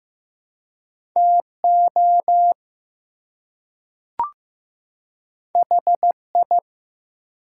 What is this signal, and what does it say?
Morse code, heard as a keyed tone, about 15 wpm, 700 Hz. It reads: TO HI
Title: Two Letter Words - 15wpm